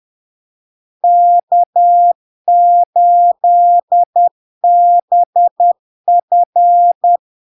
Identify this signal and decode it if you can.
Morse code, heard as a keyed tone, about 10 wpm, 700 Hz. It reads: K8BF
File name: LARK-FD-2022-Log-in-CW-10wpm-700-Hz